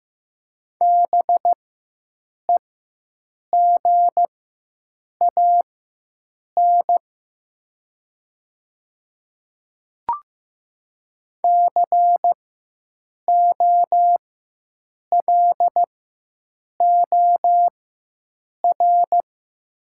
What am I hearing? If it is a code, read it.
Morse code, heard as a keyed tone, about 15 wpm, spaced as for 7 wpm, 700 Hz. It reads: BEGAN COLOR
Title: 4x Character Spacing - Top 300 Words - 15wpm